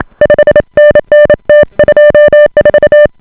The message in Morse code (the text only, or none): 5NN T24